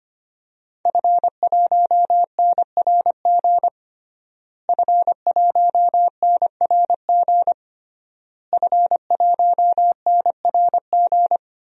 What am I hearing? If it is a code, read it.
Morse code, heard as a keyed tone, about 25 wpm, 700 Hz. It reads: F1NRG F1NRG F1NRG
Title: DXCC-100 Call Signs - 5 Characters - Repeated 3x - 25wpm